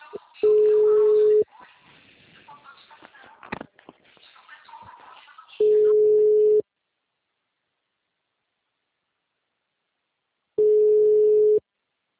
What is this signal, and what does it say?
Ringback tone (Europe, 425 Hz), with speech.